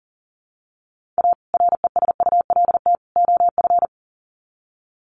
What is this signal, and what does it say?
Morse code, heard as a keyed tone, about 40 wpm, 700 Hz. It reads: A RESULT OF
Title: Sets of 3 Words 40wpm